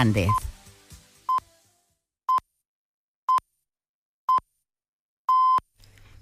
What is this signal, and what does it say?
Radio time signal (time pips), with speech. It is heard from 0.3 to 5.6 s.